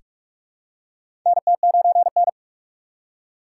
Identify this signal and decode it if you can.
Morse code, heard as a keyed tone, about 43 wpm, 700 Hz. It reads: NT9N
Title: USA Call Signs - 4 Characters - 3x Character Spacing - 45wpm